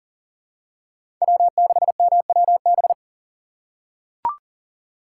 Morse code, heard as a keyed tone, about 40 wpm, 700 Hz. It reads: W6MWB
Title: USA Call Signs - 5 Characters - 2x Character Spacing - 40wpm